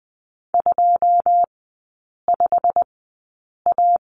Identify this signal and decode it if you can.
Morse code, heard as a keyed tone, about 20 wpm, spaced as for 8 wpm, 690 Hz. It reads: 25A
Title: 133 Lesson 22e - Question Mark - Three Characters 20wpm